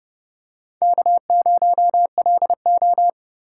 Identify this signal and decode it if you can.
Morse code, heard as a keyed tone, about 30 wpm, 700 Hz. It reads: K0LO
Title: USA Call Signs - 4 Characters - 4x Character Spacing - 30wpm